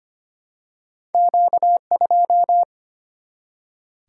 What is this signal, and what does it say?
Morse code, heard as a keyed tone, about 25 wpm, 700 Hz. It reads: Q2
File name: Two Letter-Number Combinations - 25wpm